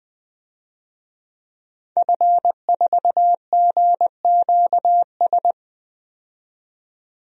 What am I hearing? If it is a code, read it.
Morse code, heard as a keyed tone, about 21 wpm, 700 Hz. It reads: F4GQS
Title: DXCC-100 Call Signs - 5 Characters - 20wpm